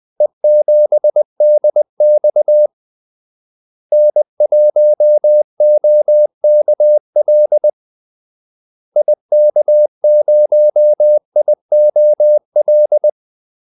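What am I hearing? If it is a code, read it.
Morse code, heard as a keyed tone, about 20 wpm, 600 Hz. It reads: E7DX N1OKL IK0IOL
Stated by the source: Morse call-sign practice at 20 wpm.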